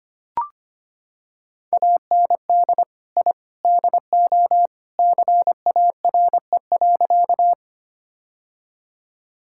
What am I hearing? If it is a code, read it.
Morse code, heard as a keyed tone, about 25 wpm, 700 Hz. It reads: AND I DO CARE.
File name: practice / 25wpm / Sentences with 3-Word Phrases - 25wpm